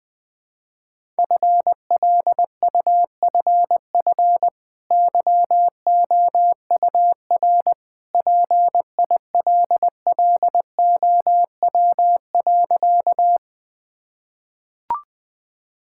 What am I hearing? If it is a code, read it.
Morse code, heard as a keyed tone, about 20 wpm, 700 Hz. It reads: FLUFF YOUR PILLOW.
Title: Easy Sentences with Hints - 20wpm